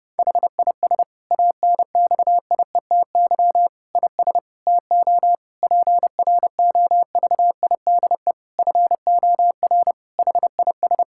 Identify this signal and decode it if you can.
Morse code, heard as a keyed tone, about 30 wpm, 700 Hz. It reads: HIS ANXIETY IS TO PROVIDE FOR HIS
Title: Sentences from Top 1600 Words 30wpm